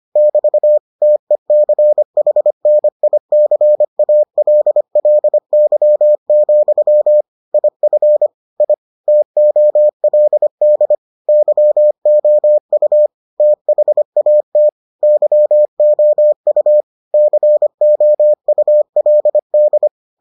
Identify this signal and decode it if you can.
Morse code, heard as a keyed tone, about 25 wpm, 600 Hz. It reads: <BT> TECHNICALLY, IF I TOLD YOU THAT YOU COULD